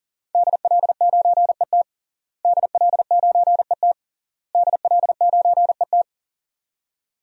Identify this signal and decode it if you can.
Morse code, heard as a keyed tone, about 40 wpm, 700 Hz. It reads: DL9ET DL9ET DL9ET
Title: DXCC-100 Call Signs - 5 Characters - Repeated 3x - 40wpm